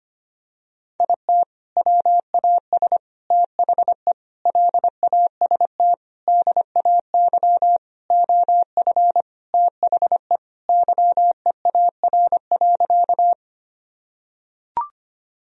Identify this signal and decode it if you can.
Morse code, heard as a keyed tone, about 25 wpm, 700 Hz. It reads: IT WAS THE LAST DAY OF THE YEAR.
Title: Sentences from Top 1000 Words - Review Part 1 - 25wpm